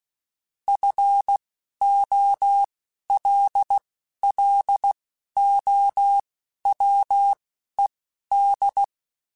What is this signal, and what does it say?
Morse code, heard as a keyed tone, about 16 wpm, spaced as for 11 wpm, 800 Hz. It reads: FOLLOWED
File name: Technical Subjects / Technical Subjects-mp3s / Sw Pwr Sup @10